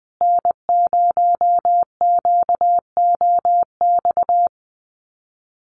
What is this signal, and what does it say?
Morse code, heard as a keyed tone, about 20 wpm, 700 Hz. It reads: N0QOX
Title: USA Call Signs Repeated 3x 20wpm